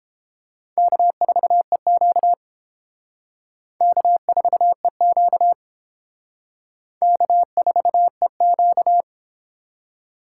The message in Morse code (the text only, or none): K4EQ K4EQ K4EQ